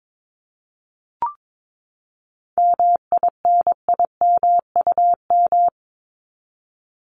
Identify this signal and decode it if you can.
Morse code, heard as a keyed tone, about 22 wpm, 700 Hz. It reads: MINIMUM
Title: Top 1251-1500 Words 22wpm